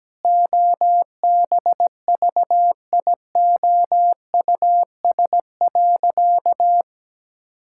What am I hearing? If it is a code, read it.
Morse code, heard as a keyed tone, about 17 wpm, 700 Hz. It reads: OBVIOUS.
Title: Sentences from Top 1600 Words 17wpm